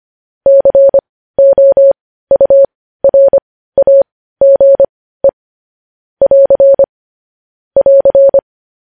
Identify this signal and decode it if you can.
Morse code, heard as a keyed tone, about 25 wpm, spaced as for 15 wpm, 550 Hz. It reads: COURAGE <AR> <AR>
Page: news headlines in morse code at 15 wpm